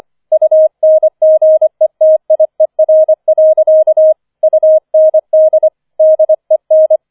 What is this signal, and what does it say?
Morse code, heard as a keyed tone, about 25 wpm, 630 Hz. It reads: UNGETIER. UND DEN
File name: 005 Fünfter Streich 25wpm